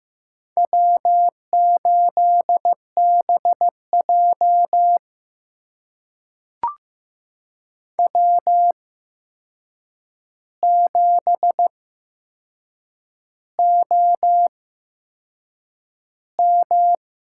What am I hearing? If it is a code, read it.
Morse code, heard as a keyed tone, about 15 wpm, spaced as for 6 wpm, 700 Hz. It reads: W8BJ W7OM
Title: USA Call Signs - 4 Characters - 8x Character Spacing - 15wpm